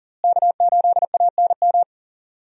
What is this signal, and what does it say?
Morse code, heard as a keyed tone, about 40 wpm, 700 Hz. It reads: K8ANM
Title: USA Call Signs - 5 Characters - Repeated 3x - 40wpm